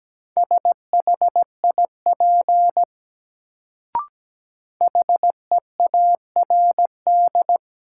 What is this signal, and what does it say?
Morse code, heard as a keyed tone, about 17 wpm, 700 Hz. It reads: SHIP HEARD
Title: Sentences from Top 400 Words - New Word Drill - 17wpm